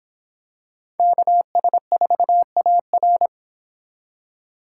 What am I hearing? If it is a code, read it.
Morse code, heard as a keyed tone, about 26 wpm, 700 Hz. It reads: KS4AR